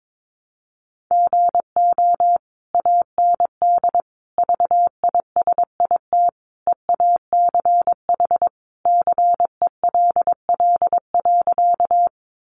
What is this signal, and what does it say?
Morse code, heard as a keyed tone, about 22 wpm, 700 Hz. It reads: GO AND VISIT EACH CELL.
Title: Sentences from Top 1100 Words 22wpm